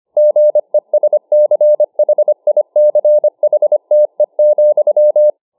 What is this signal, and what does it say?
Morse code, heard as a keyed tone, about 25 wpm, 600 Hz. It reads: GESCHICHTE,